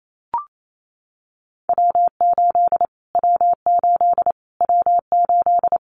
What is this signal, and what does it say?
Morse code, heard as a keyed tone, about 28 wpm, 700 Hz. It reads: W8 W8 W8
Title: Callsign Prefixes for Top 25 DXCC - Repeated 3x - 28wpm